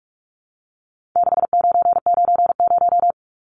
Morse code, heard as a keyed tone, about 44 wpm, 700 Hz. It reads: 6990
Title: Numbers - 4-Digit 45wpm